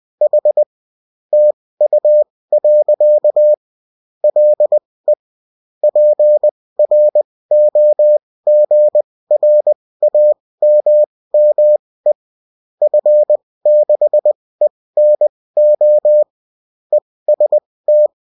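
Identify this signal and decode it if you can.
Morse code, heard as a keyed tone, about 20 wpm, spaced as for 16 wpm, 600 Hz. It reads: H TU. LE PROGRAMME F6ENO EST
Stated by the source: Bulletin in text and CW form.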